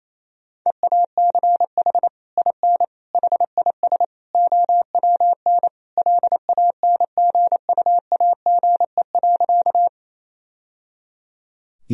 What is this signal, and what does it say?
Morse code, heard as a keyed tone, about 28 wpm, 700 Hz. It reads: EACH IN HIS OWN LANGUAGE.
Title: Sentences from Top 1000 Words - Review Part 3 - 28wpm